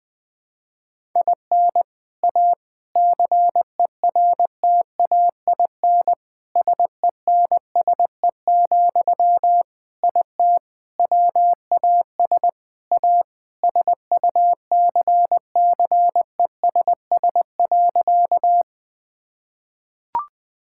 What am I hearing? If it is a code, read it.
Morse code, heard as a keyed tone, about 21 wpm, 700 Hz. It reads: IN A CERTAIN SENSE, IT WAS A SUCCESS.
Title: Sentences from Top 1000 Words - Review Part 6 - 20wpm